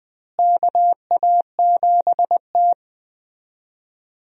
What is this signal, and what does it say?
Morse code, heard as a keyed tone, about 21 wpm, 700 Hz. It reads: KA7T
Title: USA Call Signs - 4 Characters - Repeated 2x - 20wpm